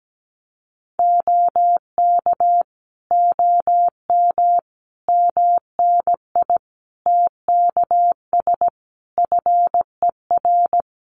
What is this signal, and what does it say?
Morse code, heard as a keyed tone, about 17 wpm, 700 Hz. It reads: OK OM MNI TKS FER
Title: Common QSO Phrases 17wpm